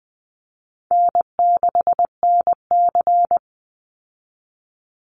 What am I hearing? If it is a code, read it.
Morse code, heard as a keyed tone, about 20 wpm, 700 Hz. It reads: N6NC
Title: Easy Call Signs Repeated 3x - 20wpm